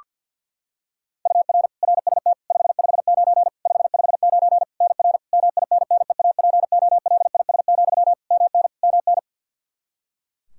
Morse code, heard as a keyed tone, about 53 wpm, 700 Hz. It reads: UR RST 559 559 NR MINNEAPOLIS, MN MN
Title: Practice QSO - Exchange 1 - Beginning - 50wpm